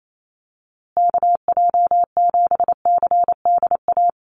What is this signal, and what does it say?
Morse code, heard as a keyed tone, about 28 wpm, 700 Hz. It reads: KJ7CDA
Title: USA Call Signs Repeated 2x 28wpm